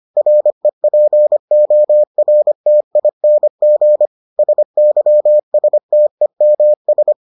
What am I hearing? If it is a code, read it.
Morse code, heard as a keyed tone, about 25 wpm, 600 Hz. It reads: REPORTING SYSTEMS